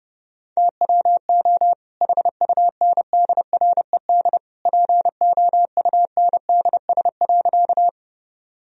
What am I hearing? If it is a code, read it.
Morse code, heard as a keyed tone, about 32 wpm, 700 Hz. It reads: TWO HUNDRED POUNDS.
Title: Sentences from Top 1000 Words - Review Part 2 - 30wpm